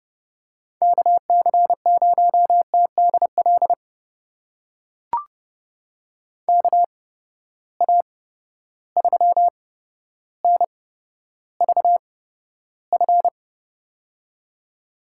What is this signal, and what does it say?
Morse code, heard as a keyed tone, about 32 wpm, spaced as for 13 wpm, 700 Hz. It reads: KC0TDL KA3NVF
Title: USA Call Signs - 6 Characters - 8x Character Spacing - 30wpm